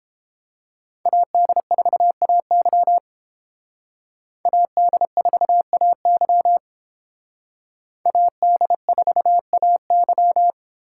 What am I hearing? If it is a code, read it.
Morse code, heard as a keyed tone, about 30 wpm, 700 Hz. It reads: AD4AY AD4AY AD4AY